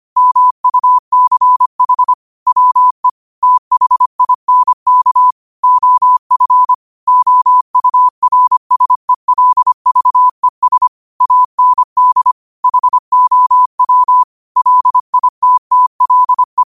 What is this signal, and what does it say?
Morse code, heard as a keyed tone, about 25 wpm, 1000 Hz. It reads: MUCH WE THINK OF OURSELVES AND HOW LITTLE
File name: qotdfiles / 25wpm